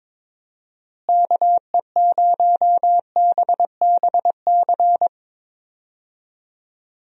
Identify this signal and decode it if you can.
Morse code, heard as a keyed tone, about 22 wpm, 700 Hz. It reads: KE0BBC
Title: USA Call Signs - 6 Characters - Repeated 2x - 22wpm